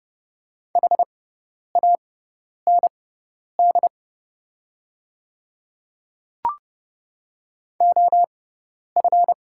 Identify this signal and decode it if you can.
Morse code, heard as a keyed tone, about 29 wpm, spaced as for 10 wpm, 690 Hz. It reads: HAND OF
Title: 6x Character Spacing - Top 200 Words - 30wpm